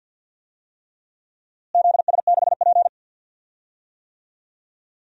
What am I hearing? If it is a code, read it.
Morse code, heard as a keyed tone, about 50 wpm, 700 Hz. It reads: ZS6P
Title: DXCC-100 Call Signs - 4 Characters - Repeated 3x - 50wpm